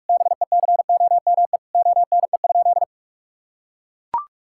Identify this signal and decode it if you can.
Morse code, heard as a keyed tone, about 43 wpm, 700 Hz. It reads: BECOME ONE?
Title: Sentences from Top 1600 Words 45wpm